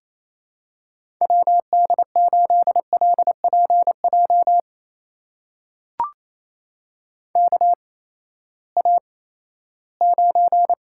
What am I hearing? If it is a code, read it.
Morse code, heard as a keyed tone, about 28 wpm, spaced as for 16 wpm, 700 Hz. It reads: WD8LPJ KA9
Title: USA Call Signs - 6 Characters - 8x Character Spacing - 28wpm